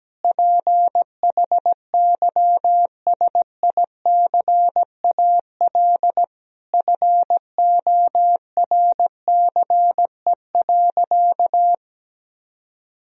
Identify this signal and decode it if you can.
Morse code, heard as a keyed tone, about 17 wpm, 700 Hz. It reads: PHYSICAL FORCE.
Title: Sentences from Top 1000 Words - Review Part 4 - 17wpm